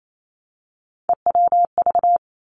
Morse code, heard as a keyed tone, about 28 wpm, 700 Hz. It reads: EWV
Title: Three Challenging Letter-Number Combinations 28wpm